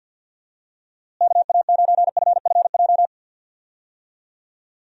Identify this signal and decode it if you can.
Morse code, heard as a keyed tone, about 52 wpm, 700 Hz. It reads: KA9FFJ
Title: USA Call Signs - 6 Characters - Repeated 3x - 50wpm